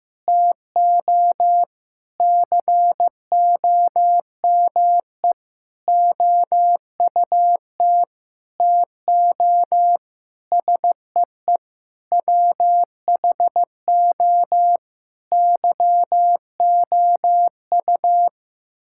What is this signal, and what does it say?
Morse code, heard as a keyed tone, about 15 wpm, 700 Hz. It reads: TO COME OUT TO SEE WHO YOU
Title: Sentences from Top 1000 Words - Review Part 6 - 15wpm